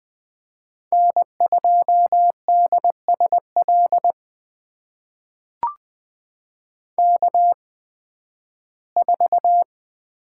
Morse code, heard as a keyed tone, about 20 wpm, spaced as for 13 wpm, 700 Hz. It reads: N2DSL K4
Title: USA Call Signs - 5 Characters - 8x Character Spacing - 20wpm